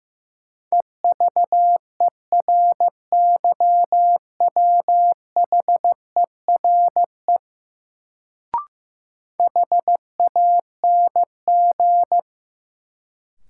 Morse code, heard as a keyed tone, about 15 wpm, 700 Hz. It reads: EVERYWHERE HANG
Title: Sentences from Top 1600 Words - New Word Drill - 15wpm